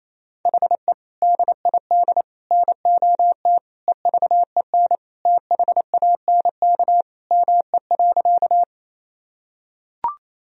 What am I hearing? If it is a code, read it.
Morse code, heard as a keyed tone, about 28 wpm, 700 Hz. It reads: HE DID NOT EVEN THANK ME.
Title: Sentences from Top 1000 Words - Review Part 1 - 28wpm